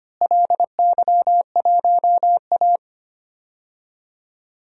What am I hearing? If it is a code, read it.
Morse code, heard as a keyed tone, about 25 wpm, 700 Hz. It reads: LY1A
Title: DXCC-100 Call Signs - 4 Characters - Repeated 3x - 25wpm